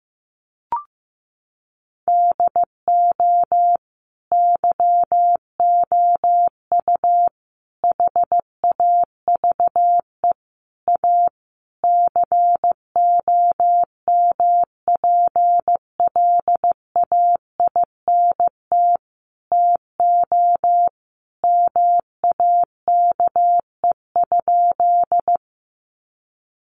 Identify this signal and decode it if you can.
Morse code, heard as a keyed tone, about 15 wpm, 700 Hz. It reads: DO YOU HAVE A COMPLAINT TO MAKE?
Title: Sentences from Top 1400 Words 15wpm